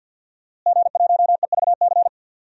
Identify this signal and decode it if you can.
Morse code, heard as a keyed tone, about 52 wpm, 700 Hz. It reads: G1EVC
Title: DXCC-100 Call Signs - 5 Characters - Repeated 3x - 50wpm